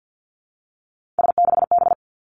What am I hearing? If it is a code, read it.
Morse code, heard as a keyed tone, about 50 wpm, 700 Hz. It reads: S6B
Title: Three Challenging Letter-Number Combinations 50wpm